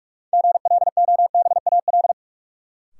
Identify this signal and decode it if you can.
Morse code, heard as a keyed tone, about 45 wpm, 700 Hz. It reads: GLOBAL
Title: Sentences from Top 1200 Words - New Word Drill Alt - 45wpm